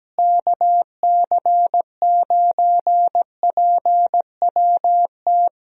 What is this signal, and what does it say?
Morse code, heard as a keyed tone, about 17 wpm, 700 Hz. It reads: KC9PWT
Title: USA Call Signs - 6 Characters - Repeated 2x - 17wpm